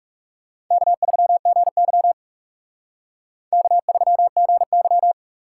Morse code, heard as a keyed tone, about 37 wpm, 700 Hz. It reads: K3GY K3GY
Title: Speed Racing - USA Call Signs 30wpm